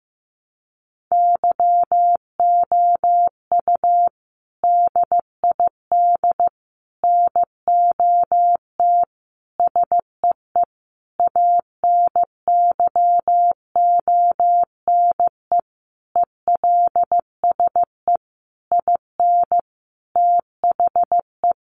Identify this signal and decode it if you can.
Morse code, heard as a keyed tone, about 15 wpm, 700 Hz. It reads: YOU DID NOT SEE ANYONE ELSE IN THE
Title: Sentences from Top 900 Words 15wpm